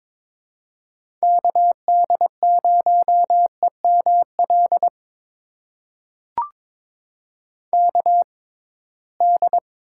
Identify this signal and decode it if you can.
Morse code, heard as a keyed tone, about 22 wpm, spaced as for 17 wpm, 700 Hz. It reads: KD0EML KD
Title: USA Call Signs - 6 Characters - 6x Character Spacing - 22wpm